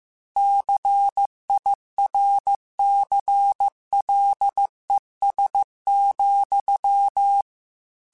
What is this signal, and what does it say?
Morse code, heard as a keyed tone, about 15 wpm, 800 Hz. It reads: CIRCLES,
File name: Technical Subjects / Technical Subjects-mp3s / XL @13